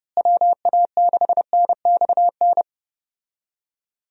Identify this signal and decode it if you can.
Morse code, heard as a keyed tone, about 30 wpm, 700 Hz. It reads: WA6NXN
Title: USA Call Signs - 6 Characters - Repeated 2x - 30wpm